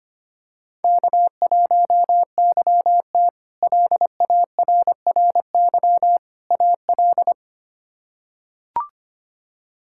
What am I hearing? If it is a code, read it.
Morse code, heard as a keyed tone, about 25 wpm, 700 Hz. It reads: K1YT LARRY AL
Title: K1USN SST QSO Practice - 25wpm